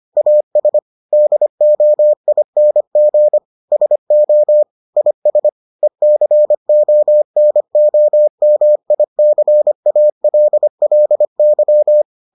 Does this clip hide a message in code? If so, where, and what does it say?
Yes: Morse code from 0.2 to 12.0 s, heard as a keyed tone, about 25 wpm, 600 Hz. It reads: AS DOING SO IS ECONOMICALLY